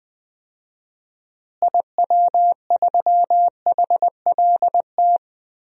Morse code, heard as a keyed tone, about 20 wpm, 700 Hz. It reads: IW3HLT
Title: DXCC-100 Call Signs - 6 Characters - Repeated 3x - 20wpm